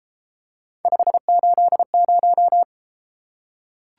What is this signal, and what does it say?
Morse code, heard as a keyed tone, about 32 wpm, 700 Hz. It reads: 580